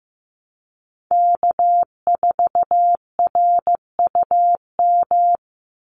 Morse code, heard as a keyed tone, about 15 wpm, 700 Hz. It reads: K4RUM
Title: Call Signs Repeated 3x - 15wpm